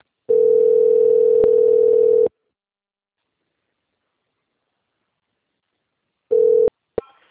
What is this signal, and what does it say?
Ringback tone (North America), 440 + 480 Hz.
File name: out- - - - - .64476